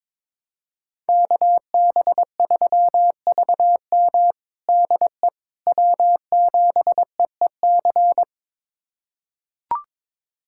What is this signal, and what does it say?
Morse code, heard as a keyed tone, about 22 wpm, 700 Hz. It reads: KB3VM DE W7EEC
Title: USA Callsign DE Callsign - 5 Characters - 22wpm